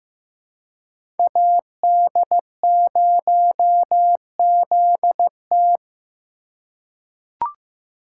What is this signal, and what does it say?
Morse code, heard as a keyed tone, about 15 wpm, 700 Hz. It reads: AD0ZT
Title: USA Call Signs - 4x Character Spacing - 15wpm